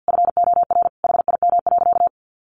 Morse code, heard as a keyed tone, about 50 wpm, 700 Hz. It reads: FOR HIM.
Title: A Sound Of Thunder 50wpm